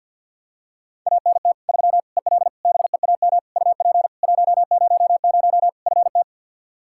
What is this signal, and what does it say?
Morse code, heard as a keyed tone, about 47 wpm, 700 Hz. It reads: ANT 3 EL BEAM UP 100 FT
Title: Practice QSO - Exchange 2 - Info - 50wpm